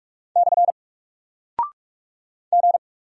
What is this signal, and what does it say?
Morse code, heard as a keyed tone, about 44 wpm, 700 Hz. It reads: / G
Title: Single Character (Letter:Number:Punctuation:Prosign) Alt 45wpm